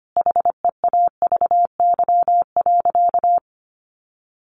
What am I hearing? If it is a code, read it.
Morse code, heard as a keyed tone, about 25 wpm, 700 Hz. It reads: HEAVY.
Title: Sentences from Top 1200 Words 25wpm